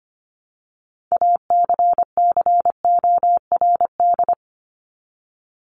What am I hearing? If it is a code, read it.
Morse code, heard as a keyed tone, about 25 wpm, 700 Hz. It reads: ACCORD
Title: ING Suffix - Top 100 Words with Base Words 25wpm